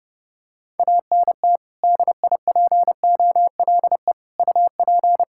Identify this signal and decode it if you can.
Morse code, heard as a keyed tone, about 30 wpm, 700 Hz. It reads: ANT DIPOLE UP
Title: K1USN SST QSO Practice 30wpm